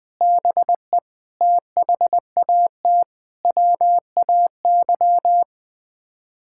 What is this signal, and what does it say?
Morse code, heard as a keyed tone, about 21 wpm, 700 Hz. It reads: BE THAT WAY
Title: Sets of 8 Words - 20wpm